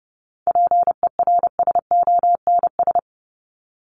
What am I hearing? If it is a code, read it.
Morse code, heard as a keyed tone, about 30 wpm, 700 Hz. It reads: PERSONS
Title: ES:S Suffix - Top 100 Words with Base Words 30wpm